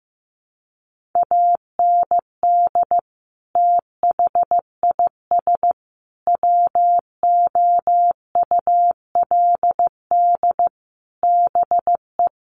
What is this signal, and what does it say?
Morse code, heard as a keyed tone, about 14 wpm, 700 Hz. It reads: AND THIS WOULD BE
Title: Sentences from Top 1400 Words 15wpm